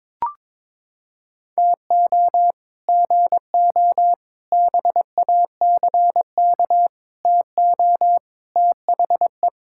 Morse code, heard as a keyed tone, about 22 wpm, 700 Hz. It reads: TO GO BACK TO THE
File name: practice / 22wpm / Sets of 7 Words - 22wpm